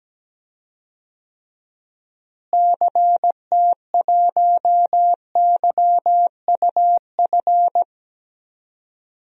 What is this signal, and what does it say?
Morse code, heard as a keyed tone, about 17 wpm, 700 Hz. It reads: CT1YUF
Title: DXCC-100 Call Signs - 6 Characters - Repeated 3x - 17wpm